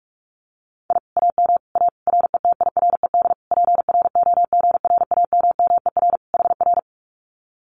Morse code, heard as a keyed tone, about 44 wpm, 700 Hz. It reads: I AM A RETIRED PROGRAMMER HR
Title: Common QSO Phrases Encore 45wpm